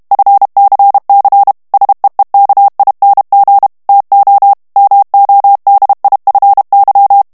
Morse code, heard as a keyed tone, about 32 wpm, 800 Hz. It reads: FCC SEEKING TO MODIFY
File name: morse mp3 / ANGLAIS texte clair 3 vitesse 30 mots minute